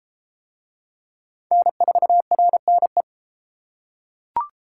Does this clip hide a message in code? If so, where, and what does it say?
Yes: Morse code from 1.5 to 3.0 s, heard as a keyed tone, about 31 wpm, 700 Hz. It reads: N4RNE